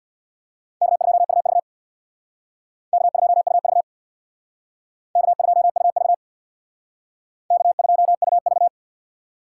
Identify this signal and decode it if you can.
Morse code, heard as a keyed tone, about 65 wpm, 700 Hz. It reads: K2UV K2UV K2UV K2UV